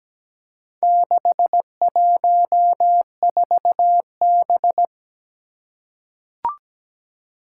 Morse code, heard as a keyed tone, about 17 wpm, 700 Hz. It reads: 614B
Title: Four Challenging Letter-Number Combinations 17wpm